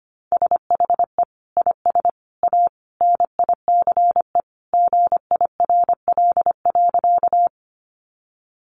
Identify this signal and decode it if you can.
Morse code, heard as a keyed tone, about 25 wpm, 700 Hz. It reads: SHE IS A NICE GIRL.
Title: Sentences from Top 1100 Words 25wpm